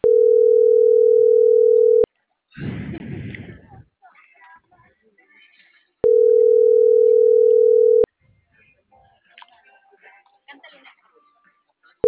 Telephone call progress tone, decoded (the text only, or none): ringback tone